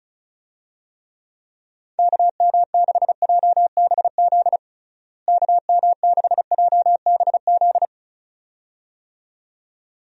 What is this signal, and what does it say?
Morse code, heard as a keyed tone, about 34 wpm, 700 Hz. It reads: KM6JBZ KM6JBZ
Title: USA Call Signs - 6 Characters - Repeated 2x - 35wpm